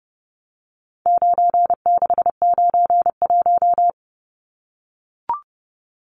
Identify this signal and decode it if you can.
Morse code, heard as a keyed tone, about 30 wpm, 700 Hz. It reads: 9691
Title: Numbers - 4-Digit 30wpm